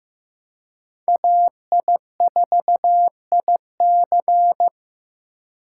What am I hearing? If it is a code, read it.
Morse code, heard as a keyed tone, about 15 wpm, 700 Hz. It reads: AI4IC
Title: USA Call Signs - 5 Characters - 2x Character Spacing - 15wpm